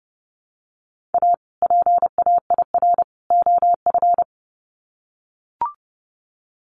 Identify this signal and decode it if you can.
Morse code, heard as a keyed tone, about 30 wpm, 700 Hz. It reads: A PAIR OF
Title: Sets of 3 Words 30wpm